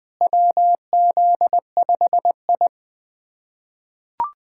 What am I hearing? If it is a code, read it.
Morse code, heard as a keyed tone, about 20 wpm, 700 Hz. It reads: WZ5I
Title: USA Callsign DE Callsign - 4 Characters - 20wpm